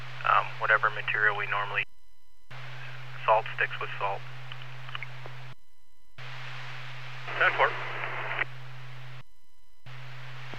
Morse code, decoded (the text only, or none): none